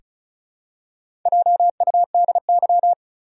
Morse code, heard as a keyed tone, about 34 wpm, 700 Hz. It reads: JUDY
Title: USA Common Names - Max 4 Letters - 35wpm